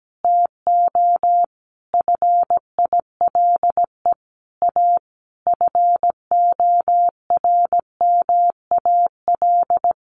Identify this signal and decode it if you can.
Morse code, heard as a keyed tone, about 17 wpm, 700 Hz. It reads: TO FILE A FORMAL
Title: Sentences from Top 1300 Words 17wpm